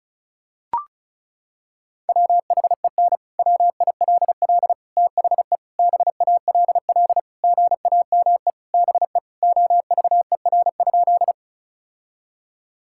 Morse code, heard as a keyed tone, about 36 wpm, 700 Hz. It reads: WHEN WILL THE BALL GAME BE OVER?
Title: Sentences from Top 1000 Words - Review Part 6 - 35wpm